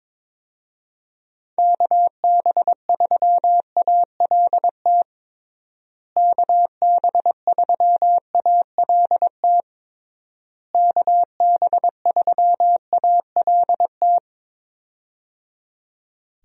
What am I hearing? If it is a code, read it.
Morse code, heard as a keyed tone, about 23 wpm, 700 Hz. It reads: KB3ALT KB3ALT KB3ALT
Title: USA Call Signs - 6 Characters - Repeated 3x - 22wpm